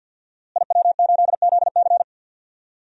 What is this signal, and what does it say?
Morse code, heard as a keyed tone, about 50 wpm, 700 Hz. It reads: IW8ZC
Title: DXCC-100 Call Signs - 5 Characters - Repeated 3x - 50wpm